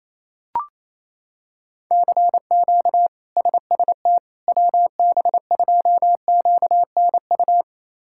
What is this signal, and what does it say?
Morse code, heard as a keyed tone, about 28 wpm, 700 Hz. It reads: CQ SST WB2QNU
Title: K1USN SST QSO Practice - 28wpm